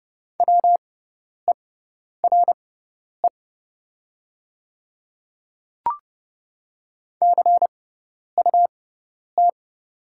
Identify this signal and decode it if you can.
Morse code, heard as a keyed tone, about 30 wpm, spaced as for 10 wpm, 700 Hz. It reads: WERE CUT
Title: 6x Character Spacing - Top 500 Words - 30wpm